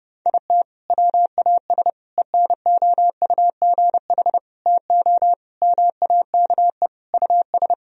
Morse code, heard as a keyed tone, about 32 wpm, 700 Hz. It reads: IT WAS ENOUGH TO MAKE US
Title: Sentences from Top 1000 Words - Review Part 4 - 30wpm